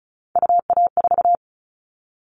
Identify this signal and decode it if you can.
Morse code, heard as a keyed tone, about 34 wpm, 700 Hz. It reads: UA4
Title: Callsign Prefixes for Top 50 DXCC - 35wpm